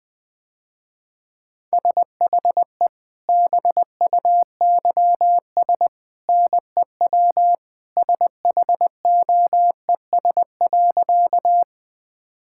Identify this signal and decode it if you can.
Morse code, heard as a keyed tone, about 20 wpm, 700 Hz. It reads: SHE BUYS NEW SHOES.
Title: Easy Sentences with Hints - 20wpm